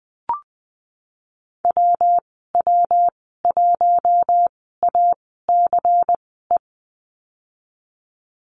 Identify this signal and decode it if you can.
Morse code, heard as a keyed tone, about 19 wpm, spaced as for 14 wpm, 700 Hz. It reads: WW1ACE
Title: 122 Lesson 20g - Number 3 - Call Signs 20wpm